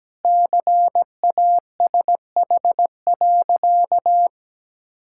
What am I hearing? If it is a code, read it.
Morse code, heard as a keyed tone, about 17 wpm, 700 Hz. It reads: CASH.
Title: Sentences from Top 1000 Words - Review Part 5 - 17wpm